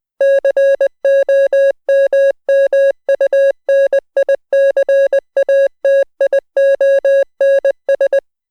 Morse code, heard as a keyed tone, about 20 wpm, 550 Hz. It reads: COMMUNICATIONS